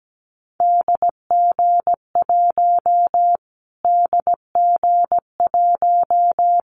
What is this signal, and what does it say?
Morse code, heard as a keyed tone, about 17 wpm, 700 Hz. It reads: DG1 DG1
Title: Callsign Prefixes for Top 25 DXCC - Repeated 3x - 17wpm